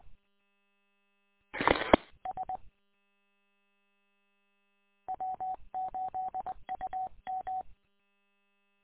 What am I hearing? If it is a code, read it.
Morse code, heard as a keyed tone, about 21 wpm, 750 Hz. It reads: EE S W8UM